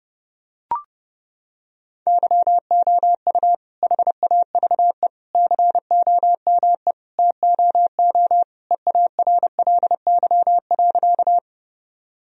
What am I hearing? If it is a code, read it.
Morse code, heard as a keyed tone, about 30 wpm, 700 Hz. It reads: YOU HAVE COME TOO EARLY.
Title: Sentences from Top 1000 Words - Review Part 2 - 30wpm